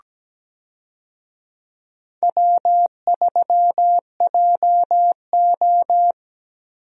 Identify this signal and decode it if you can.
Morse code, heard as a keyed tone, about 17 wpm, 700 Hz. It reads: W3JO
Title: USA Call Signs - 4 Characters - 17wpm